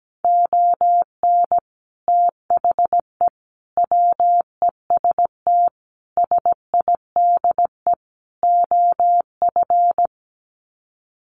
Morse code, heard as a keyed tone, about 17 wpm, 700 Hz. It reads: ON THE WEST SIDE OF
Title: Sets of 5 Words Encore2 17wpm